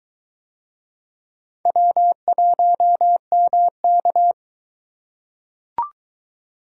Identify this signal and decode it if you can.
Morse code, heard as a keyed tone, about 23 wpm, 700 Hz. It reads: W1MK